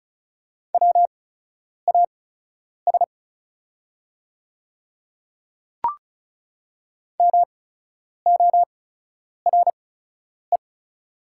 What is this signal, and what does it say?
Morse code, heard as a keyed tone, about 35 wpm, spaced as for 10 wpm, 700 Hz. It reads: WAS MORE
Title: 8x Character Spacing - Top 100 Words - 35wpm